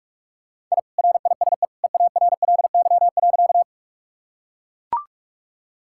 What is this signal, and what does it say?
Morse code, heard as a keyed tone, about 44 wpm, 700 Hz. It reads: I RISE EARLY.
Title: Sentences from Top 1000 Words - Review Part 6 - 45wpm